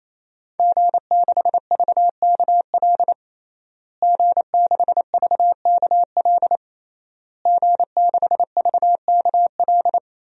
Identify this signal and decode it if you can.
Morse code, heard as a keyed tone, about 28 wpm, 700 Hz. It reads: G6VKL G6VKL G6VKL
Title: DXCC-100 Call Signs - 5 Characters - Repeated 3x - 28wpm